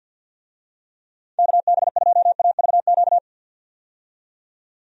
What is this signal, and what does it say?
Morse code, heard as a keyed tone, about 51 wpm, 700 Hz. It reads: KB2AVQ
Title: USA Call Signs - 6 Characters - 50wpm